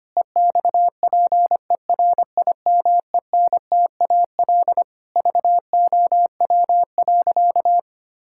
Morse code, heard as a keyed tone, about 25 wpm, 700 Hz. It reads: EXPERIMENTAL VOW.
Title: Pangram Challenge v2 - 25wpm